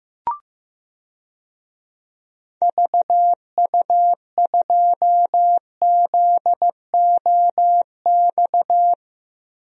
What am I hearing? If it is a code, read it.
Morse code, heard as a keyed tone, about 15 wpm, 700 Hz. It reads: VU2ZOX
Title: DXCC-100 Call Signs - 6 Characters - Repeated 2x - 15wpm